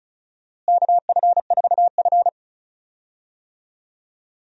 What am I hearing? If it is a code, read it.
Morse code, heard as a keyed tone, about 34 wpm, 700 Hz. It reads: KF4F
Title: USA Call Signs - 4 Characters - Repeated 3x - 35wpm